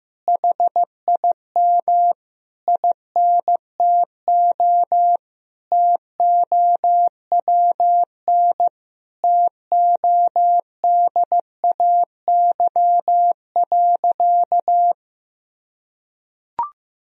Morse code, heard as a keyed tone, about 15 wpm, 700 Hz. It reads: HIM INTO TOWN TODAY.
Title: Sentences from Top 1000 Words - Review Part 3 - 15wpm